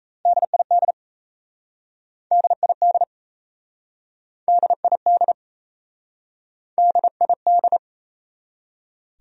Morse code, heard as a keyed tone, about 37 wpm, 700 Hz. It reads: DID DID DID DID